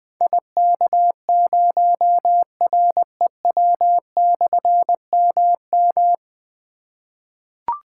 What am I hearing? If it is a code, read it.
Morse code, heard as a keyed tone, about 20 wpm, 700 Hz. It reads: IK0REW/MM
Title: Difficult Call Signs 20wpm